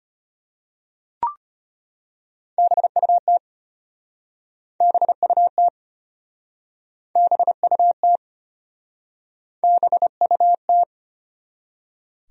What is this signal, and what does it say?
Morse code, heard as a keyed tone, about 33 wpm, 700 Hz. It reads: BUT BUT BUT BUT